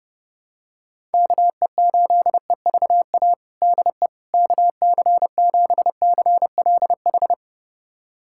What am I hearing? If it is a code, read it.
Morse code, heard as a keyed tone, about 29 wpm, 700 Hz. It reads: KE8EVA DE KC7CLH
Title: USA Callsign DE Callsign 30wpm